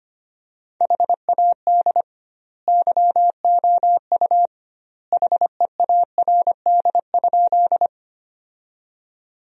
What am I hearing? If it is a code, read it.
Morse code, heard as a keyed tone, about 25 wpm, 700 Hz. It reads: HAD YOU HEARD?
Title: 2x Word-Spacing - Sentences from Top 400 Words - 25wpm